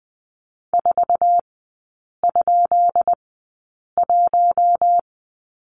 Morse code, heard as a keyed tone, about 20 wpm, spaced as for 8 wpm, 700 Hz. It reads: 4?1
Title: 133 Lesson 22e - Question Mark - Three Characters 20wpm